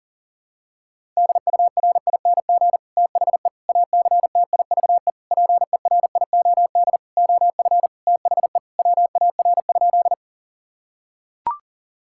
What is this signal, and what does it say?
Morse code, heard as a keyed tone, about 40 wpm, 700 Hz. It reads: DURING THE ACTIVE PERIOD OF THE WAR?
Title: Sentences from Top 1000 Words - Review Part 6 - 40wpm